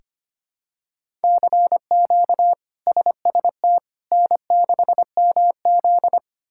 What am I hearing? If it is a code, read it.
Morse code, heard as a keyed tone, about 25 wpm, 700 Hz. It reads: CQ SST N6MZ
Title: K1USN SST QSO Practice - 25wpm